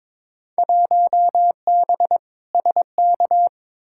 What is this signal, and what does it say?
Morse code, heard as a keyed tone, about 22 wpm, 700 Hz. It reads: 1B SK
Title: Field Day Exchanges Encore - 22wpm